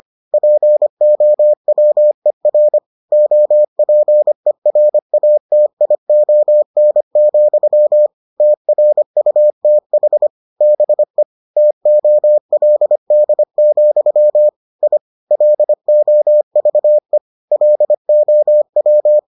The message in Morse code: POWER OPERATION, TRUTH BE TOLD, I LOVE LOW

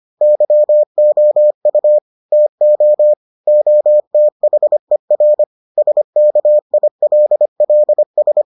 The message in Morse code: YOU TO OTHER SKILLS

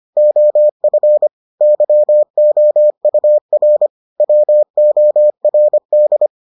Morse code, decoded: OF YOUR WORD

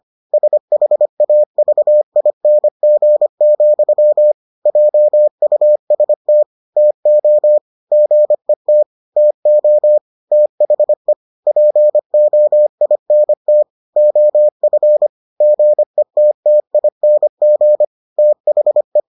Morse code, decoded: SHAVING, JUST TO GET TO THE POINT OF GETTING THE

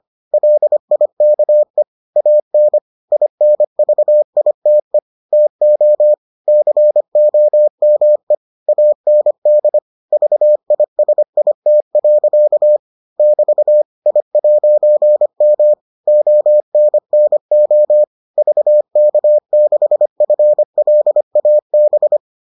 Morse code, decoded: LIKE AN INVITE TO COME AND VISIT. <BT> I'M ONNO VK6FLAB

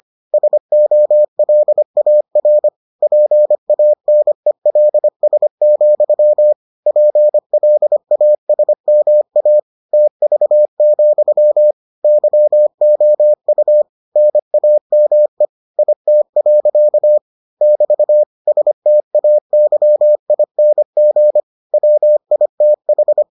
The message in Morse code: SOLAR PANELS, PLASMA TV, YOU NAME IT. <BT> STAYING WITH

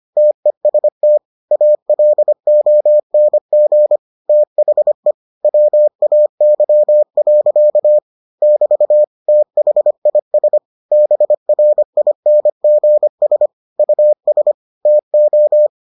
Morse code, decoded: TEST ALONG THE WAY. <BT> THIS BRINGS US TO

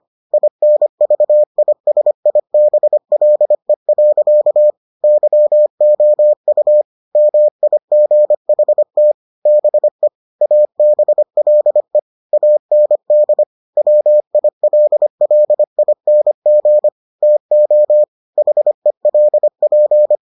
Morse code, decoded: INVISIBLE. YOU MIGHT BE ABLE AND WILLING TO HELP

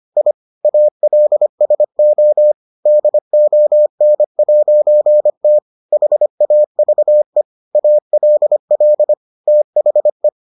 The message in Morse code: I ALSO DON'T HAVE ALL THE

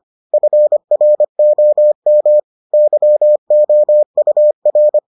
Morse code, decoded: FROM YOUR

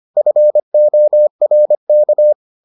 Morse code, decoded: FORK

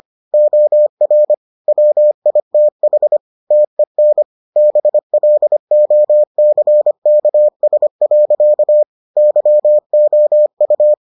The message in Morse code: OR WITH TEN BLOCKS. YOU

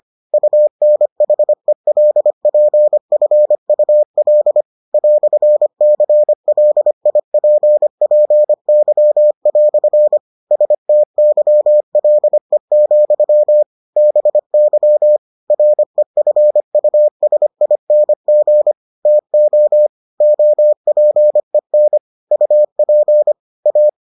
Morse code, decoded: UNHELPFUL "CLIPPY" STYLE, BY REFUSING TO OPEN UP A

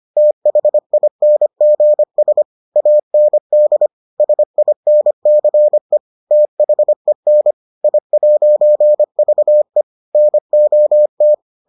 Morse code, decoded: THINGS AND SINCE THEN I'VE NOT

